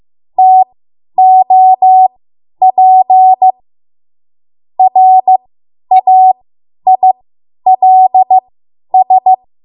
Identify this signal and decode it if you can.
Morse code, heard as a keyed tone, about 15 wpm, spaced as for 10 wpm, 750 Hz. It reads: TOP RAILS
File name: Morse / 10WPM